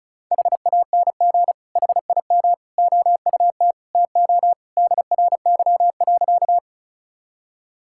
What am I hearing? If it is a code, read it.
Morse code, heard as a keyed tone, about 36 wpm, 700 Hz. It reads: HANG HIM OUT TO DRY.
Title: Sentences from Top 1600 Words 35wpm